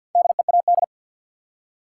Morse code, heard as a keyed tone, about 47 wpm, 700 Hz. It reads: DEAD
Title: Four Letter Words - 50wpm